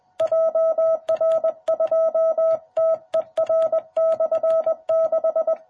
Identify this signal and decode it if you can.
Morse code, heard as a keyed tone, about 20 wpm, 660 Hz. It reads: JR2TER/6
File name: jh8cbh / 2021cw